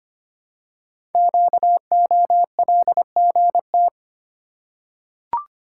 Morse code, heard as a keyed tone, about 25 wpm, 700 Hz. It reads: QOLGT
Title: Code Groups - 5 Letters - 25wpm